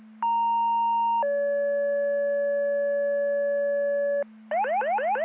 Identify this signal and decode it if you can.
Two-tone page, heard as a one-second tone, then a three-second tone, about 930 then 570 Hz, with noise. It is heard from 0.2 to 4.2 s.